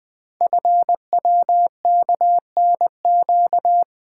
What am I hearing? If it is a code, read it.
Morse code, heard as a keyed tone, about 20 wpm, 700 Hz. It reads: FWKNQ
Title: Code Groups - 5 Letters - 20wpm